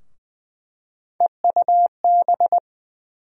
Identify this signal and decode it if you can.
Morse code, heard as a keyed tone, about 20 wpm, 700 Hz. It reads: EUB
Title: USA Call Sign Postfixes - 3 Letters 20wpm